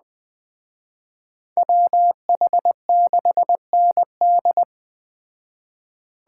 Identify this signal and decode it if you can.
Morse code, heard as a keyed tone, about 20 wpm, 700 Hz. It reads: WH6ND
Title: Speed Racing - USA Call Signs 17wpm